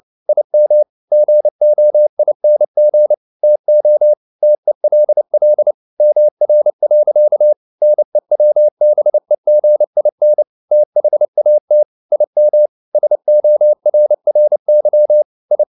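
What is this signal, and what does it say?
Morse code, heard as a keyed tone, about 29 wpm, 600 Hz. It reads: IM GOING TO TELL MR. NEWBEGIN THAT IM SORRY I
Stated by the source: Morse code at 29 wpm, sending Ken Follett's novel Never in English, from CW med Gnister.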